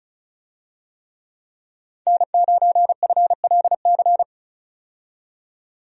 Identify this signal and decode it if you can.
Morse code, heard as a keyed tone, about 34 wpm, 700 Hz. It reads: N9FLC
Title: USA Call Signs - 5 Characters - 35wpm